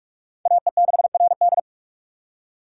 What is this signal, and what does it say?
Morse code, heard as a keyed tone, about 46 wpm, 700 Hz. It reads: AE6RD
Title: USA Call Signs - 5 Characters - 2x Character Spacing - 45wpm